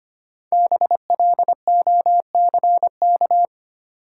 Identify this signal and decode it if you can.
Morse code, heard as a keyed tone, about 25 wpm, 700 Hz. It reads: BLOCK
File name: 5 Letter Words - First 3 Letters Given - 25wpm